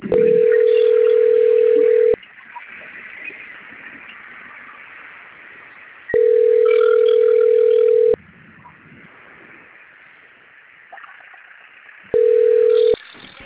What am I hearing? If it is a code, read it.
Ringback tone (North America), 440 + 480 Hz.